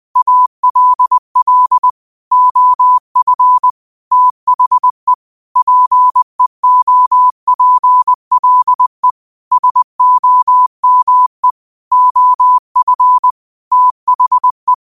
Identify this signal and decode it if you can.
Morse code, heard as a keyed tone, about 20 wpm, 1000 Hz. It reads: ALL OF THE PEOPLE SOME OF THE
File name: qotdfiles / 20wpm